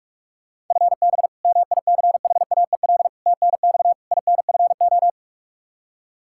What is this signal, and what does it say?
Morse code, heard as a keyed tone, about 47 wpm, 700 Hz. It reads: FB MICHAEL TNX INFO
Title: Practice QSO - Exchange 3 - More Info - 45wpm